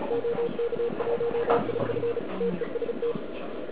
Morse code, heard as a keyed tone, about 36 wpm, 470 Hz, with noise. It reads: OK1IF TEST
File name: ARCHIV / KX3 / KV PA